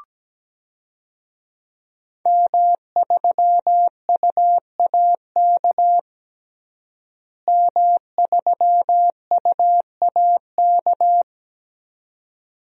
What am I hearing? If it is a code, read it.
Morse code, heard as a keyed tone, about 17 wpm, 700 Hz. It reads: M3UAK M3UAK
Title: DXCC-100 Call Signs - 5 Characters - Repeated 2x - 17wpm